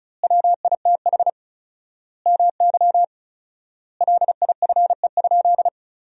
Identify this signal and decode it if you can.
Morse code, heard as a keyed tone, about 36 wpm, 700 Hz. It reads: WITH MY LIFE?
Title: 4x Word-Spacing - Sentences from Top 300 Words - 35wpm